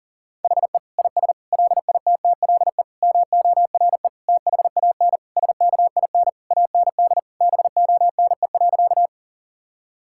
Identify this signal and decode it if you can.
Morse code, heard as a keyed tone, about 40 wpm, 700 Hz. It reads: HE IS LITTLE MORE THAN SKIN AND BONE.
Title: Sentences from Top 1600 Words 40wpm